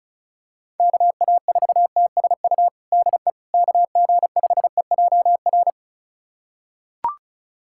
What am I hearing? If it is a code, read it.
Morse code, heard as a keyed tone, about 36 wpm, 700 Hz. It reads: KA4TSU DE KG5EJR
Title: USA Callsign DE Callsign - 6 Characters - 35wpm